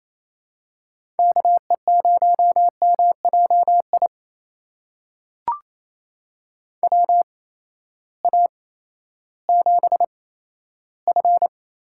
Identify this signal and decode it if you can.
Morse code, heard as a keyed tone, about 28 wpm, spaced as for 14 wpm, 700 Hz. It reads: KE0MJI WA7F
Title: USA Call Signs - 6 Characters - 8x Character Spacing - 28wpm